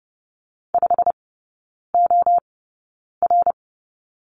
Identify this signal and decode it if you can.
Morse code, heard as a keyed tone, about 30 wpm, spaced as for 9 wpm, 700 Hz. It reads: 5OR
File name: Lesson 16e - Letter c - Three Characters 30wpm